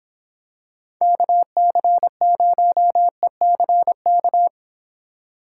Morse code, heard as a keyed tone, about 24 wpm, 700 Hz. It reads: KC0ECK